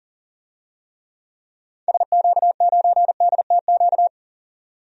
Morse code, heard as a keyed tone, about 37 wpm, 700 Hz. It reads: SQ9DTQ
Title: DXCC-100 Call Signs - 6 Characters - 40wpm